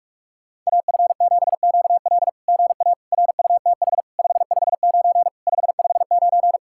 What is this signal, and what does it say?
Morse code, heard as a keyed tone, about 44 wpm, 690 Hz. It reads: AF7QL GA RUTH 559 559
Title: SOTA QSO Practice 45wpm